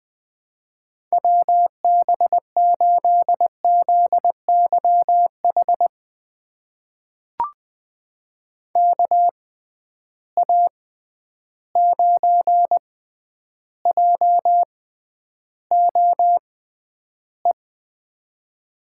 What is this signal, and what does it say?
Morse code, heard as a keyed tone, about 21 wpm, spaced as for 10 wpm, 700 Hz. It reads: WB8ZYH KA9JOE
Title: USA Call Signs - 6 Characters - 6x Character Spacing - 20wpm